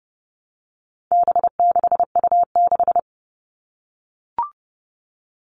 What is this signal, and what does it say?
Morse code, heard as a keyed tone, about 30 wpm, 700 Hz. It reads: B6U6
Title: All Cut Numbers - 4-Digit 30wpm